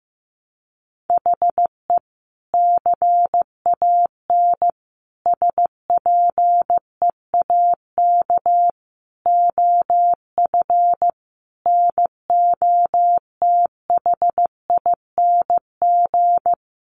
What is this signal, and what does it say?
Morse code, heard as a keyed tone, about 15 wpm, 700 Hz. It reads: HE CAN SPEAK OF NOTHING
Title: Sentences from Top 1100 Words 15wpm